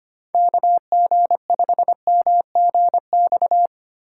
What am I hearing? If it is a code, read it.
Morse code, heard as a keyed tone, about 25 wpm, 700 Hz. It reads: KG5MGX
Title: USA Call Signs - 6 Characters - 4x Character Spacing - 25wpm